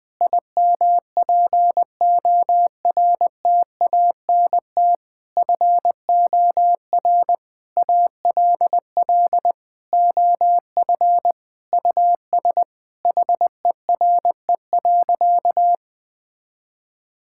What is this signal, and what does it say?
Morse code, heard as a keyed tone, about 20 wpm, 700 Hz. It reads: IMPORTANT FOR ALL OF US HERE.
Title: Sentences with 5-Word Phrases - 20wpm